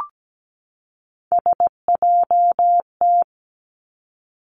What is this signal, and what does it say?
Morse code, heard as a keyed tone, about 17 wpm, 700 Hz. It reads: SJT
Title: Three Challenging Letter-Number Combinations 17wpm